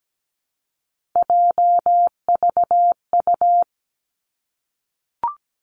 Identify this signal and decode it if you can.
Morse code, heard as a keyed tone, about 17 wpm, 700 Hz. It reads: JVU
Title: Three Challenging Letter-Number Combinations 17wpm